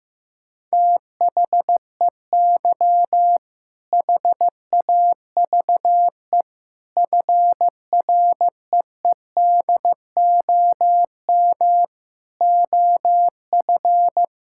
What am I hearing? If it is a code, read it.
Morse code, heard as a keyed tone, about 16 wpm, 700 Hz. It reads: THEY HAVE FREEDOM OF
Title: Sentences from Top 1000 Words - Review Part 5 - 15wpm